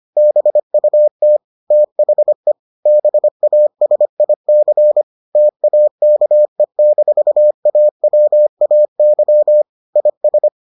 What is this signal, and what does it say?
Morse code, heard as a keyed tone, about 25 wpm, 600 Hz. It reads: BUT THE BASIC TAKE-AWAY IS